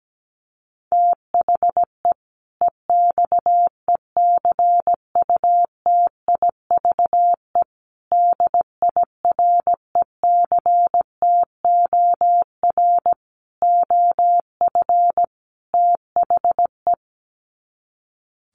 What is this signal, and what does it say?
Morse code, heard as a keyed tone, about 17 wpm, 700 Hz. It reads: THE EXECUTIVE DIRECTOR OF THE
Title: Sets of 5 Words Encore 17wpm